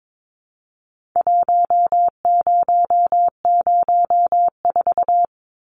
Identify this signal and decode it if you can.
Morse code, heard as a keyed tone, about 22 wpm, 700 Hz. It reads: 1004
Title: Numbers - 4-Digit 22wpm